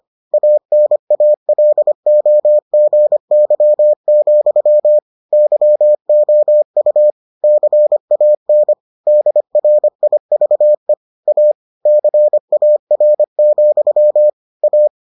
Morse code, heard as a keyed tone, about 25 wpm, 600 Hz. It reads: ANALOGY, YOU CAN DRIVE A CAR, A